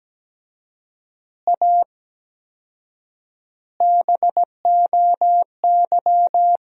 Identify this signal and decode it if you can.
Morse code, heard as a keyed tone, about 17 wpm, 700 Hz. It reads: A BOY
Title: 4x Word-Spacing - Sentences from Top 400 Words - 17wpm